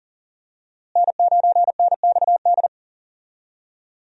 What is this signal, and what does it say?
Morse code, heard as a keyed tone, about 40 wpm, 700 Hz. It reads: N9NXD
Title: USA Call Signs - 5 Characters - Repeated 2x - 40wpm